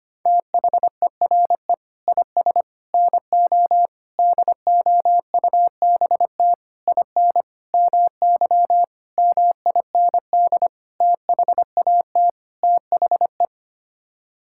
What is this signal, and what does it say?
Morse code, heard as a keyed tone, about 25 wpm, 700 Hz. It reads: THERE IS NO DOUBT IN MY MIND THAT THE
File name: Sets of 9 Words - 25wpm